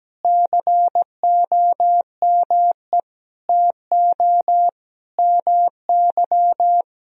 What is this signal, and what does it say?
Morse code, heard as a keyed tone, about 17 wpm, 700 Hz. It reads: COME TO MY
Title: Sentences from Top 400 Words and Abbreviations 17wpm